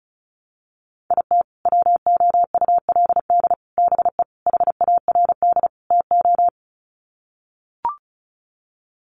Morse code, heard as a keyed tone, about 34 wpm, 700 Hz. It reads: IT WOULD BE HARD TO
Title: Sets of 5 Words Encore2 35wpm